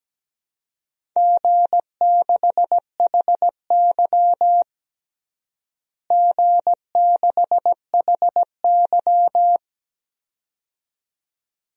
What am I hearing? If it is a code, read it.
Morse code, heard as a keyed tone, about 17 wpm, 700 Hz. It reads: G6HY G6HY
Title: DXCC-100 Call Signs - 4 Characters - Repeated 2x - 17wpm